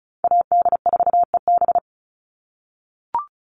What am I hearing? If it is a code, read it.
Morse code, heard as a keyed tone, about 36 wpm, 700 Hz. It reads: AD4EB
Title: Call Signs - 35wpm